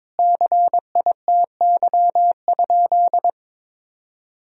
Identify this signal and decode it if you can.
Morse code, heard as a keyed tone, about 22 wpm, 700 Hz. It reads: CITY?
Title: Sentences from Top 1000 Words - Review Part 1 - 22wpm